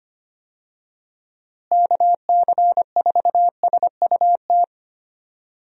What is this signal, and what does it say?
Morse code, heard as a keyed tone, about 25 wpm, 700 Hz. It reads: KC4SUT
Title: USA Call Signs - 6 Characters - Repeated 2x - 25wpm